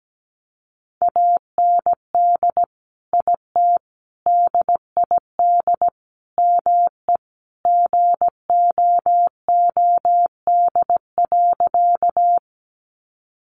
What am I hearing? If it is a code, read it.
Morse code, heard as a keyed tone, about 17 wpm, 700 Hz. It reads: AND IT DID ME GOOD.
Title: Sentences from Top 200 Words 17wpm